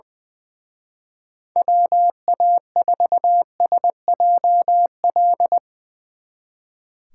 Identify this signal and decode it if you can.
Morse code, heard as a keyed tone, about 22 wpm, 700 Hz. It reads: WA4SJL
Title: Speed Racing - USA 2x3 Call Signs 20wpm